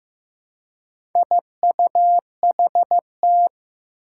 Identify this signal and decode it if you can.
Morse code, heard as a keyed tone, about 15 wpm, 700 Hz. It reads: IUHT
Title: Four Challenging Letter-Number Combinations 15wpm